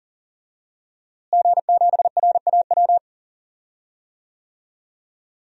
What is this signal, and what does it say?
Morse code, heard as a keyed tone, about 40 wpm, 700 Hz. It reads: G7RAW
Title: DXCC-100 Call Signs - 5 Characters - Repeated 3x - 40wpm